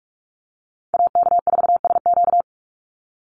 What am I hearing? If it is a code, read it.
Morse code, heard as a keyed tone, about 44 wpm, 700 Hz. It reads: AK4SQ
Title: Call Signs - 45wpm